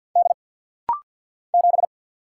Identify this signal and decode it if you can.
Morse code, heard as a keyed tone, about 50 wpm, 700 Hz. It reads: D 7
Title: Single Letter-Number Alt - Rapid-Fire 50wpm